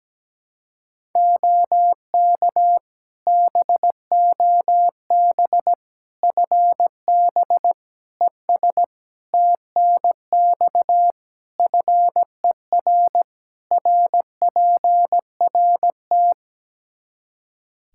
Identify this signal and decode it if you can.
Morse code, heard as a keyed tone, about 17 wpm, 700 Hz. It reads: OK BOB FB ES TNX FER RPRT
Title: Practice QSO - Exchange 2 - Info - 17wpm